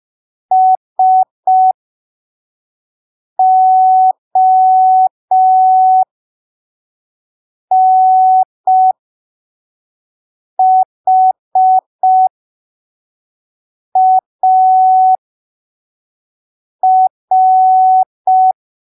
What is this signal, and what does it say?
Morse code, heard as a keyed tone, about 5 wpm, spaced as for 3 wpm, 750 Hz. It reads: SONHAR
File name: arquivos / texto / texto 21 05ppm